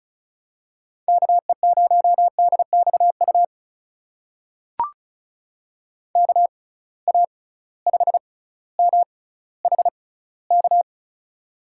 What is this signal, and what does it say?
Morse code, heard as a keyed tone, about 36 wpm, spaced as for 18 wpm, 700 Hz. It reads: KE0DXU KA5MHK
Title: USA Call Signs - 6 Characters - 6x Character Spacing - 35wpm